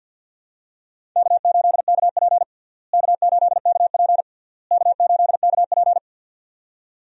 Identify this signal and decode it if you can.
Morse code, heard as a keyed tone, about 52 wpm, 700 Hz. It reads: K8KP K8KP K8KP
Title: USA Call Signs - 4 Characters - Repeated 3x - 50wpm